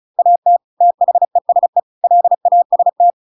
Morse code, heard as a keyed tone, about 34 wpm, 700 Hz. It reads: AT THESE LAST